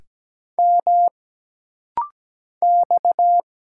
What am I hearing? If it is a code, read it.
Morse code, heard as a keyed tone, about 17 wpm, 700 Hz. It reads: M X
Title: Single Letters - Rapid-Fire 17wpm